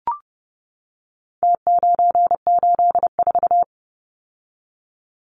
Morse code, heard as a keyed tone, about 30 wpm, 700 Hz. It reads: T984
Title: Numbers - Contest Style - 4-Digit 30wpm